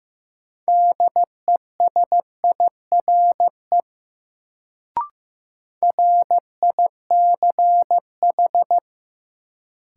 Morse code, heard as a keyed tone, about 15 wpm, 700 Hz. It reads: DESIRE RICH
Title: Sentences from Top 1100 Words - New Word Drill Alt - 15wpm